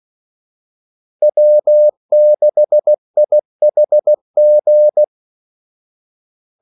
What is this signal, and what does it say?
Morse code, heard as a keyed tone, about 16 wpm, 600 Hz. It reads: W6IHG
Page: Kaldesignaler 16 wpm